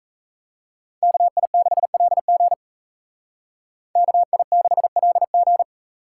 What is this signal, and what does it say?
Morse code, heard as a keyed tone, about 36 wpm, 700 Hz. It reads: KI6LG KI6LG